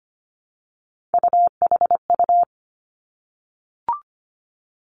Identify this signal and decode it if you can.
Morse code, heard as a keyed tone, about 25 wpm, 700 Hz. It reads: UHU
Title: Three Challenging Letter-Number Combinations 25wpm